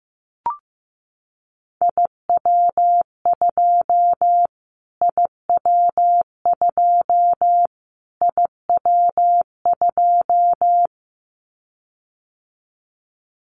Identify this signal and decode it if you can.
Morse code, heard as a keyed tone, about 15 wpm, 700 Hz. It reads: IW2 IW2 IW2
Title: Callsign Prefixes for Top 25 DXCC - Repeated 3x - 15wpm